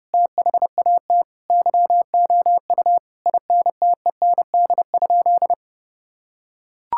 Morse code, heard as a keyed tone, about 28 wpm, 700 Hz. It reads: THAT YOU INTEND?
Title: Sentences from Top 1600 Words 30wpm